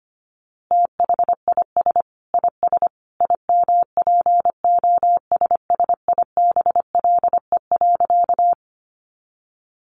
Morse code, heard as a keyed tone, about 25 wpm, 700 Hz. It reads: THIS IS IMPOSSIBLE.
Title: A Sound Of Thunder 25wpm